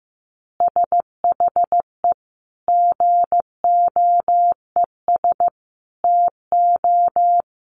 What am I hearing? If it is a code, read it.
Morse code, heard as a keyed tone, about 15 wpm, 700 Hz. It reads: SHE GOES TO
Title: Additional Easy Phrases 15wpm